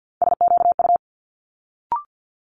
Morse code, heard as a keyed tone, about 50 wpm, 700 Hz. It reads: SQU
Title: Multi-letter Phonemes Quick Fire 50wpm